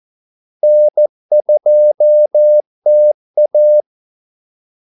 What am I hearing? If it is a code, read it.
Morse code, heard as a keyed tone, about 14 wpm, 600 Hz. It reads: N2TA
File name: Callsigns-14wpm